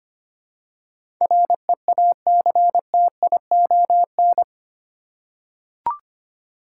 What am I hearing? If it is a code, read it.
Morse code, heard as a keyed tone, about 25 wpm, 700 Hz. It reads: REACTION
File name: Eight Letter Words - 25wpm